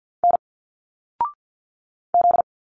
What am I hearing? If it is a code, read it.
Morse code, heard as a keyed tone, about 47 wpm, 700 Hz. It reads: N Z
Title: Single Letters 50wpm